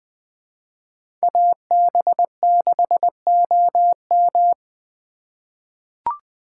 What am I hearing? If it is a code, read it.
Morse code, heard as a keyed tone, about 20 wpm, 700 Hz. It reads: AB6OM
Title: USA Call Signs - 5 Characters - 2x Character Spacing - 20wpm